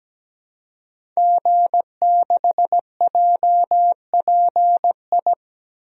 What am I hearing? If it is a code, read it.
Morse code, heard as a keyed tone, about 17 wpm, 700 Hz. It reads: G6JPI
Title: DXCC-100 Call Signs - 5 Characters - Repeated 3x - 17wpm